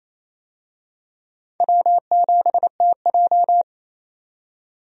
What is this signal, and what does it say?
Morse code, heard as a keyed tone, about 28 wpm, 700 Hz. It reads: W7TJ
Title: USA Call Signs - 4 Characters - 28wpm